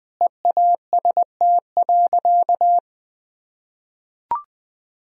Morse code, heard as a keyed tone, about 20 wpm, 700 Hz. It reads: EAST.
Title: Sentences from Top 1000 Words - Review Part 3 - 20wpm